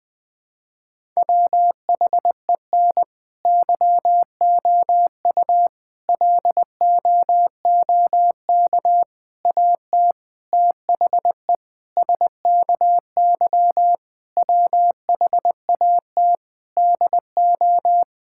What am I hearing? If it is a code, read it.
Morse code, heard as a keyed tone, about 20 wpm, 700 Hz. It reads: WHEN YOU LOOK AT THE SKY WHAT DO
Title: Sentences with 3-Word Phrases - 20wpm